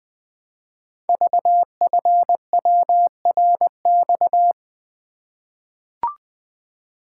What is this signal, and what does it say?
Morse code, heard as a keyed tone, about 20 wpm, 700 Hz. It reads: VFWRX
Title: Code Groups - 5 Letters - 20wpm